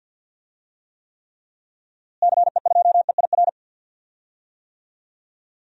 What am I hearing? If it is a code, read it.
Morse code, heard as a keyed tone, about 52 wpm, 700 Hz. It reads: CE2EIR
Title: DXCC-100 Call Signs - 6 Characters - Repeated 3x - 50wpm